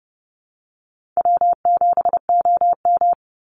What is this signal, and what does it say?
Morse code, heard as a keyed tone, about 30 wpm, 700 Hz. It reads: W7OM
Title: Easy Call Signs Repeated 2x - 30wpm